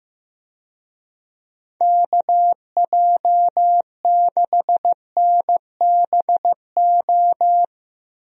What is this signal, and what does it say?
Morse code, heard as a keyed tone, about 15 wpm, 700 Hz. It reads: KJ6NBO
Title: USA Call Signs - 6 Characters - Repeated 2x - 15wpm